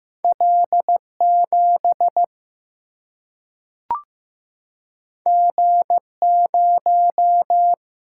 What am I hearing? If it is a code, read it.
Morse code, heard as a keyed tone, about 15 wpm, 700 Hz. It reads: L7 G0
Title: Single-Letter Single-Number 15wpm